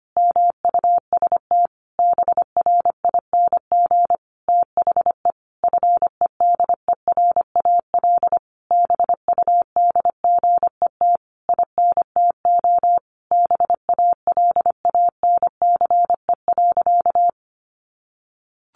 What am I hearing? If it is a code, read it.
Morse code, heard as a keyed tone, about 25 wpm, 700 Hz. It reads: MUST BRING THE FEDERAL BUDGET INTO BALANCE.
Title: Sentences from Top 1200 Words 25wpm